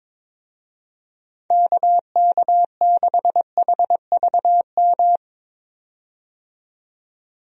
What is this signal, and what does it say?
Morse code, heard as a keyed tone, about 21 wpm, 700 Hz. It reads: KK6HVM
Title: USA Call Signs - 6 Characters - 22wpm